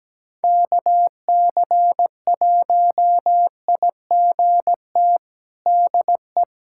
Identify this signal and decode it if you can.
Morse code, heard as a keyed tone, about 17 wpm, 700 Hz. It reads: KC1IGT DE
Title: USA Callsign DE Callsign - 6 Characters - 17wpm